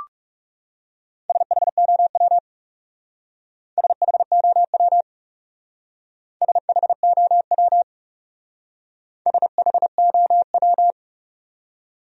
Morse code, heard as a keyed tone, about 36 wpm, 700 Hz. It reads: SHOW SHOW SHOW SHOW
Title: Speed-Racing Top 200 Words 30wpm